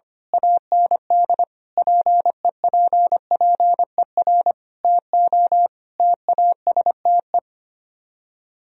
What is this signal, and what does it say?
Morse code, heard as a keyed tone, about 24 wpm, 700 Hz. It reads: AND PEPPER TO TASTE
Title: Speed-Racing - Sets of 5 Words 25wpm